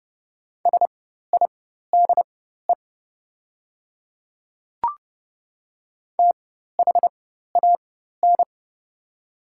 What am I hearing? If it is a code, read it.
Morse code, heard as a keyed tone, about 30 wpm, spaced as for 14 wpm, 700 Hz. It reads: SIDE THAN
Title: 4x Character Spacing - Top 200 Words - 30wpm